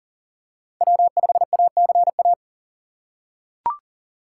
Morse code, heard as a keyed tone, about 40 wpm, 700 Hz. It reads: W5ACA
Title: USA Call Signs - 4x Character Spacing - 40wpm